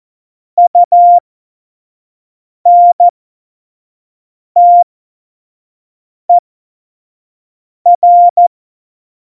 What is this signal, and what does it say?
Morse code, heard as a keyed tone, about 14 wpm, spaced as for 5 wpm, 700 Hz. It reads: UNTER
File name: Morsekurs / Witze / Witz 08. Dezember2020, 25 F70